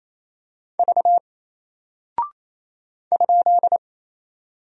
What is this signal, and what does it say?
Morse code, heard as a keyed tone, about 27 wpm, 700 Hz. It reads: V ?
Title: Single Character (Letter:Number:Punctuation:Prosign) Alt 28wpm